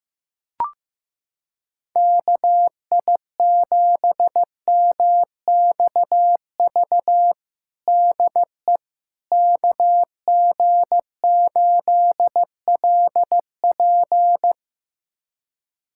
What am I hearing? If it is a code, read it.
Morse code, heard as a keyed tone, about 16 wpm, 700 Hz. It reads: KI7MXV DE KG8LP
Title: USA Callsign DE Callsign 15wpm